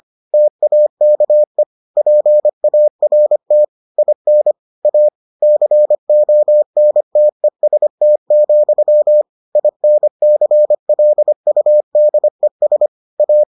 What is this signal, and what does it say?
Morse code, heard as a keyed tone, about 25 wpm, 600 Hz. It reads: TAKE PART IN A CONTEST, INCLUDES A